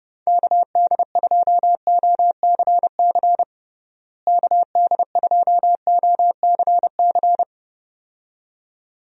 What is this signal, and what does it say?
Morse code, heard as a keyed tone, about 30 wpm, 700 Hz. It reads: KD2OCC KD2OCC
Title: USA Call Signs - 6 Characters - Repeated 2x - 30wpm